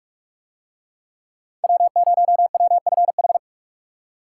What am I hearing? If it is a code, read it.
Morse code, heard as a keyed tone, about 40 wpm, 700 Hz. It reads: W0WFH